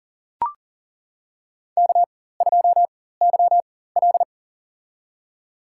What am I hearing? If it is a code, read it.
Morse code, heard as a keyed tone, about 40 wpm, spaced as for 19 wpm, 700 Hz. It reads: K2YL
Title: USA Call Signs - 4x Character Spacing - 40wpm